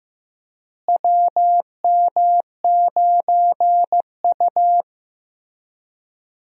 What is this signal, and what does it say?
Morse code, heard as a keyed tone, about 15 wpm, 700 Hz. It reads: WM9U
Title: USA Call Signs - 4 Characters - Repeated 2x - 15wpm